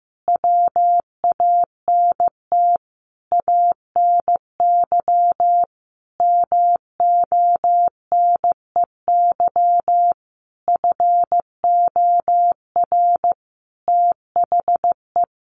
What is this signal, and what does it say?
Morse code, heard as a keyed tone, about 15 wpm, 700 Hz. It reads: WANT ANY MONEY FOR THE
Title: Sentences from Top 1400 Words 15wpm